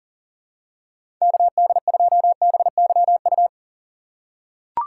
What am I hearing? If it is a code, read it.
Morse code, heard as a keyed tone, about 40 wpm, 700 Hz. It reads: KD2BYU
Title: USA Call Signs - 6 Characters - 6x Character Spacing - 40wpm